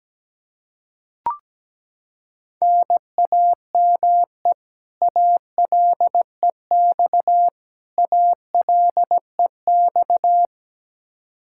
Morse code, heard as a keyed tone, about 17 wpm, 700 Hz. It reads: NAME ALEX ALEX
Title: Practice QSO - Exchange 1 - Beginning - 17wpm